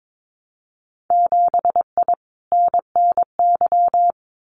Morse code, heard as a keyed tone, about 22 wpm, 700 Hz. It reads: 7I NNY
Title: Winter Field Day Exchanges Encore 22wpm